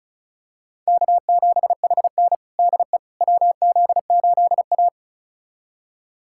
Morse code, heard as a keyed tone, about 36 wpm, 700 Hz. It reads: K7HN DE WZ8A
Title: USA Callsign DE Callsign - 4 Characters - 35wpm